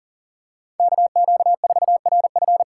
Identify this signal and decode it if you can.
Morse code, heard as a keyed tone, about 40 wpm, 700 Hz. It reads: KQ4RF
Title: USA Call Signs - 5 Characters - 3x Character Spacing - 40wpm